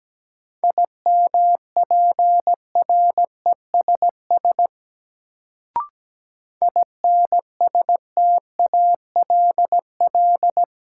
Morse code, heard as a keyed tone, about 17 wpm, 700 Hz. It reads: IMPRESS INSTALL
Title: Sentences from Top 1600 Words - New Word Drill Alt - 17wpm